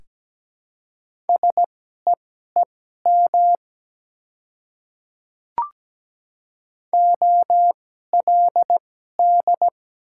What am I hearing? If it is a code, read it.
Morse code, heard as a keyed tone, about 17 wpm, spaced as for 12 wpm, 700 Hz. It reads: SEEM OLD
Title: 2x Character Spacing - Top 300 Words - 17wpm